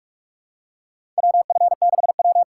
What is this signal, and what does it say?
Morse code, heard as a keyed tone, about 46 wpm, 690 Hz. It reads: WF6W
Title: USA Call Signs - 4 Characters - 8x Character Spacing - 45wpm